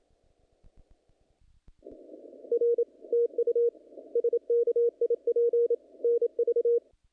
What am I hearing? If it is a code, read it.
Morse code, heard as a keyed tone, about 28 wpm, 470 Hz. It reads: R TU SKIP NV
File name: contests / naqp-cw-k5tr-op-k5pi- -1900